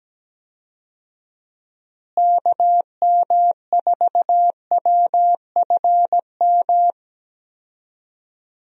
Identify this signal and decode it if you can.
Morse code, heard as a keyed tone, about 17 wpm, 700 Hz. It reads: KM4WFM
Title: USA Call Signs - 6 Characters - Repeated 3x - 17wpm